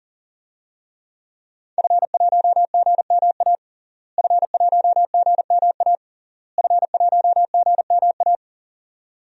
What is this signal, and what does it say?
Morse code, heard as a keyed tone, about 40 wpm, 700 Hz. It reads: F1GMA F1GMA F1GMA
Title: DXCC-100 Call Signs - 5 Characters - Repeated 3x - 40wpm